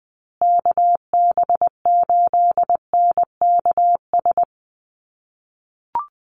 Morse code, heard as a keyed tone, about 20 wpm, 700 Hz. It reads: KB8NKS
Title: USA Call Signs Repeated 3x 20wpm